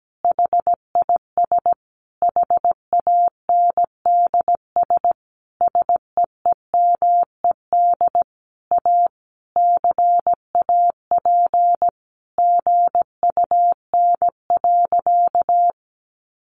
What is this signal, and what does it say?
Morse code, heard as a keyed tone, about 17 wpm, 700 Hz. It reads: HIS HANDS SEEMED A CAP GUN.
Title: A Sound Of Thunder 17wpm